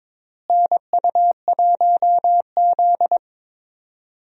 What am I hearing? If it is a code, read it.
Morse code, heard as a keyed tone, about 22 wpm, 700 Hz. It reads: NU1Z
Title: USA Call Signs - 4 Characters - Repeated 2x - 22wpm